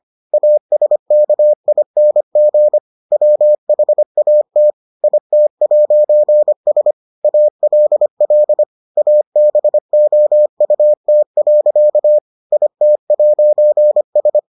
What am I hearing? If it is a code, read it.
Morse code, heard as a keyed tone, about 25 wpm, 600 Hz. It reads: ASKING WHAT IT'S ALL ABOUT. IT'S